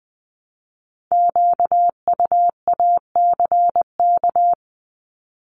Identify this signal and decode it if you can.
Morse code, heard as a keyed tone, about 21 wpm, 700 Hz. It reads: QUACK
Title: Multi-letter Phonemes Repeated 3x 20wpm